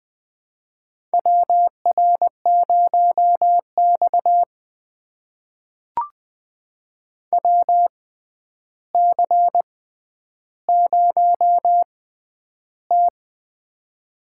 Morse code, heard as a keyed tone, about 20 wpm, spaced as for 10 wpm, 700 Hz. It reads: WR0X WC0T
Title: USA Call Signs - 4 Characters - 6x Character Spacing - 20wpm